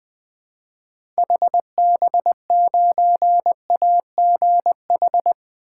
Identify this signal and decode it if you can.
Morse code, heard as a keyed tone, about 20 wpm, 700 Hz. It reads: HB9AGH
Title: 2023 SOTA Chaser Call Signs - 20wpm